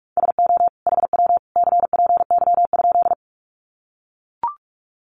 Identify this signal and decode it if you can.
Morse code, heard as a keyed tone, about 44 wpm, 700 Hz. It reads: SO HW CPY?
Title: Common QSO Phrases 45wpm